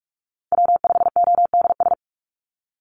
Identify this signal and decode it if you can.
Morse code, heard as a keyed tone, about 46 wpm, 700 Hz. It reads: W5ODS
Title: NA SOTA Chaser Callsigns 45wpm